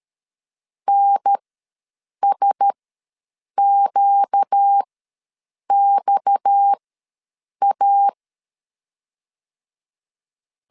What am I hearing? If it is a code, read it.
Morse code, heard as a keyed tone, about 13 wpm, spaced as for 7 wpm, 800 Hz. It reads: NSQXA
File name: LEZIONI / 6 PKX0 / 6 PKX ws 12wpm 3106